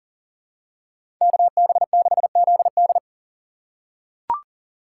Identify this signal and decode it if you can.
Morse code, heard as a keyed tone, about 40 wpm, 700 Hz. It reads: KB6ZD
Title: USA Call Signs - 5 Characters - 2x Character Spacing - 40wpm